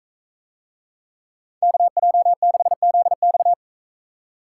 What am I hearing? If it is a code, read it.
Morse code, heard as a keyed tone, about 40 wpm, 700 Hz. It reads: KJ6ZX